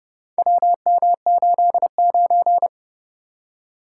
Morse code, heard as a keyed tone, about 30 wpm, 700 Hz. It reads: WM89